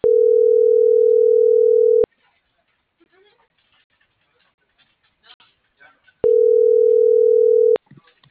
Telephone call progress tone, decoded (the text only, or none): ringback tone